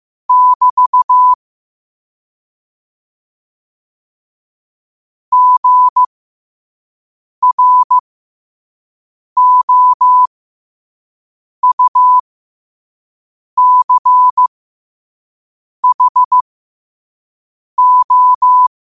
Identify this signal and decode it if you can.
Morse code, heard as a keyed tone, about 15 wpm, spaced as for 5 wpm, 1000 Hz. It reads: <BT> GROUCHO
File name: qotdfiles / 5wpm